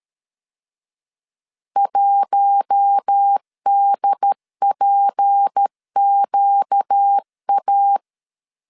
Morse code, heard as a keyed tone, about 13 wpm, 800 Hz. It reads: 1DPQA
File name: CW / DET / 1 6 1500C S A 12wpm 3440